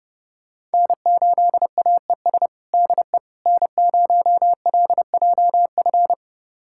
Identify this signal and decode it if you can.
Morse code, heard as a keyed tone, about 30 wpm, 700 Hz. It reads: N8AES DE N0LJF
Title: USA Callsign DE Callsign - 5 Characters - 30wpm